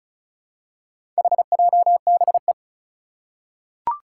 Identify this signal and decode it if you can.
Morse code, heard as a keyed tone, about 36 wpm, 700 Hz. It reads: HJBE
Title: Four Challenging Letter-Number Combinations 35wpm